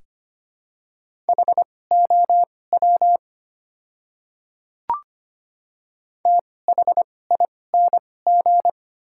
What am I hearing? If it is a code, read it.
Morse code, heard as a keyed tone, about 25 wpm, spaced as for 18 wpm, 690 Hz. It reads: HOW THING
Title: 2x Character Spacing - Top 100 Words - 25wpm